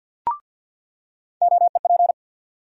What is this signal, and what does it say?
Morse code, heard as a keyed tone, about 49 wpm, 700 Hz. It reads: OEP
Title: USA Call Sign Postfixes - 3 Letters 50wpm